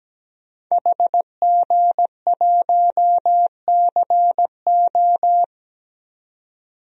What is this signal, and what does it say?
Morse code, heard as a keyed tone, about 17 wpm, 700 Hz. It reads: HG1CO
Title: DXCC-100 Call Signs - 5 Characters - 17wpm